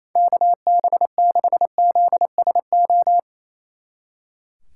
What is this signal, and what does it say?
Morse code, heard as a keyed tone, about 34 wpm, 700 Hz. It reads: KB6ZSO
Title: Speed Racing - USA 2x3 Call Signs 28wpm